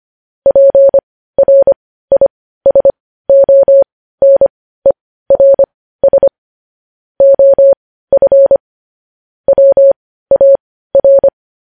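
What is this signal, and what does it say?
Morse code, heard as a keyed tone, about 25 wpm, spaced as for 15 wpm, 550 Hz. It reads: PRISONERS OF WAR
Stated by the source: Morse code news headlines at 15 WPM.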